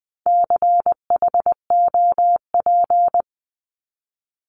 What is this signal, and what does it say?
Morse code, heard as a keyed tone, about 20 wpm, 700 Hz. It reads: CHOP
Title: 260 Lesson 42e - Words 20-20wpm